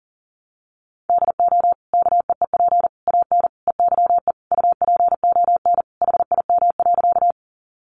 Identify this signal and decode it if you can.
Morse code, heard as a keyed tone, about 40 wpm, 700 Hz. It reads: DO KEEP AN EYE UPON HIM.
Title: Sentences from Top 1100 Words 40wpm